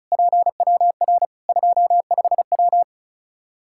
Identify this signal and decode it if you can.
Morse code, heard as a keyed tone, about 34 wpm, 700 Hz. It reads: PWR 25W
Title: Practice QSO - Exchange 2 - Info - 35wpm